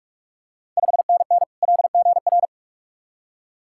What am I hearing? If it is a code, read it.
Morse code, heard as a keyed tone, about 42 wpm, 690 Hz. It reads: 5NN LGR
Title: Ontario QSO Party Exchange 45wpm